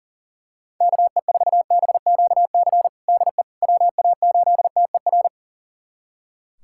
Morse code, heard as a keyed tone, about 40 wpm, 700 Hz. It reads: KE4BQC DE WA8TER
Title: USA Callsign DE Callsign 40wpm